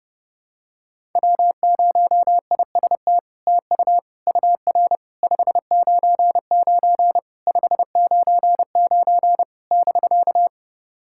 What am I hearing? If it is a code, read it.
Morse code, heard as a keyed tone, about 32 wpm, 700 Hz. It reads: W0IST TU UR 599 599 <BK>
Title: SOTA QSO Practice 30wpm